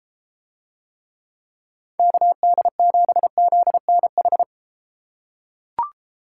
Morse code, heard as a keyed tone, about 32 wpm, 700 Hz. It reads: KD7ZNH